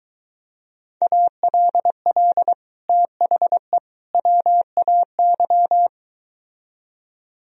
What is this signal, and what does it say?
Morse code, heard as a keyed tone, about 23 wpm, 700 Hz. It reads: ALL THE WAY